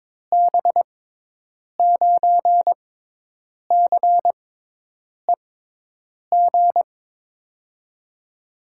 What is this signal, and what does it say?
Morse code, heard as a keyed tone, about 22 wpm, spaced as for 8 wpm, 700 Hz. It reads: B9CEG
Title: USA Call Signs - 6 Characters - 6x Character Spacing - 22wpm